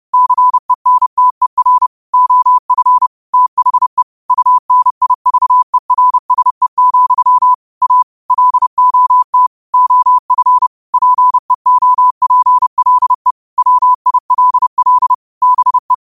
Morse code, heard as a keyed tone, about 30 wpm, 1000 Hz. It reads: CENTER OF THE UNIVERSE, A LOT OF PEOPLE WILL BE